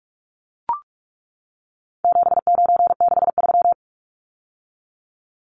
Morse code, heard as a keyed tone, about 44 wpm, 700 Hz. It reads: 7963
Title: Numbers - 4-Digit 45wpm